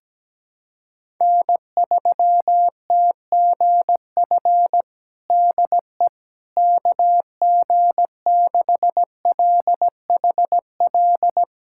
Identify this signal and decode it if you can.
Morse code, heard as a keyed tone, about 17 wpm, 700 Hz. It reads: N3TGF DE KG6LHL
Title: USA Callsign DE Callsign 17wpm